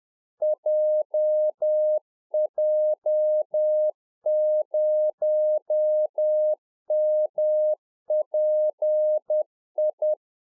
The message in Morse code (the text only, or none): JJ0MPI